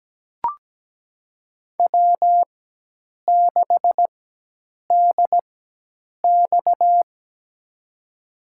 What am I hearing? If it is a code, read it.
Morse code, heard as a keyed tone, about 17 wpm, spaced as for 8 wpm, 700 Hz. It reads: W6DX
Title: USA Call Signs - 4 Characters - 4x Character Spacing - 17wpm